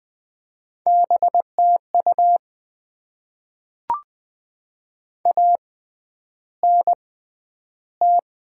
Morse code, heard as a keyed tone, about 20 wpm, 700 Hz. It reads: B T U A N T
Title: Top 20 QSO Elements - 6x Character Spacing - 20wpm